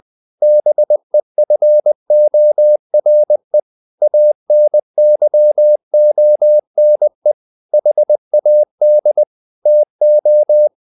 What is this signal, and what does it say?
Morse code, heard as a keyed tone, about 20 wpm, 600 Hz. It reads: BEFORE ANYONE HAD TO
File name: the old man and the sea part 2 20wpm